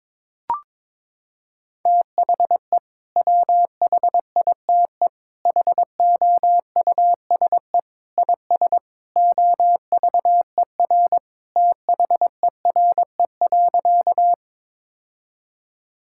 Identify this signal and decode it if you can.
Morse code, heard as a keyed tone, about 22 wpm, 700 Hz. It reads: THE WHITE HOUSE IS OVER THERE.
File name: practice / 22wpm / Sentences with 3-Word Phrases - 22wpm